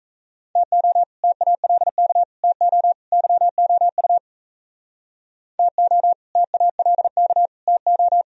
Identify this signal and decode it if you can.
Morse code, heard as a keyed tone, about 34 wpm, 700 Hz. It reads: TO TALK TO YOU TO TALK TO
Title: Speed-Racing - Sets of 4 Words 28wpm